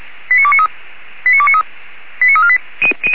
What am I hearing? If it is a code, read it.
Five-tone selective call (ZVEI), heard as quick runs of five tones, tones of about 70 ms. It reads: 89293 89293 89348
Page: ZVEI